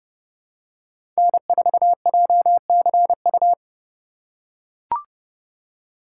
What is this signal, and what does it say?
Morse code, heard as a keyed tone, about 30 wpm, 700 Hz. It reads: N4JCU
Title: USA Call Signs - 5 Characters - 4x Character Spacing - 30wpm